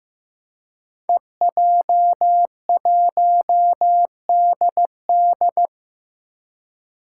Morse code, heard as a keyed tone, about 15 wpm, 700 Hz. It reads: EJ1DD
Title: Five Challenging Letter-Number Combinations 15wpm